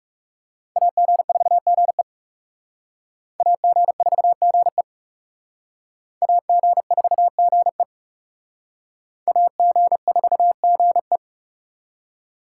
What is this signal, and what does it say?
Morse code, heard as a keyed tone, about 37 wpm, 700 Hz. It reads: AG4GE AG4GE AG4GE AG4GE
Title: Speed Racing - USA 2x2 Call Signs 30wpm